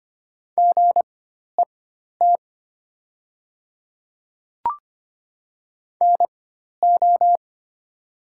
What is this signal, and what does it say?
Morse code, heard as a keyed tone, about 25 wpm, spaced as for 12 wpm, 700 Hz. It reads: GET NO
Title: 4x Character Spacing - Top 200 Words - 25wpm